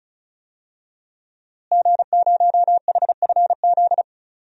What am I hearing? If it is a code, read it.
Morse code, heard as a keyed tone, about 34 wpm, 700 Hz. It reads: G0HFZ
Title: DXCC-100 Call Signs - 5 Characters - Repeated 3x - 35wpm